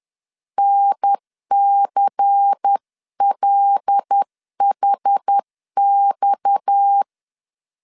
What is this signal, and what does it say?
Morse code, heard as a keyed tone, about 10 wpm, 800 Hz. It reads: NCLHX
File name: CW / DET / 1 6 1500C S C 10wpm 4252